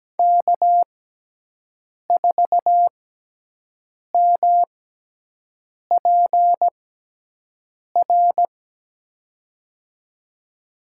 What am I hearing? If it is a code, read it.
Morse code, heard as a keyed tone, about 17 wpm, spaced as for 6 wpm, 700 Hz. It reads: K4MPR
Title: USA Call Signs - 6x Character Spacing - 17wpm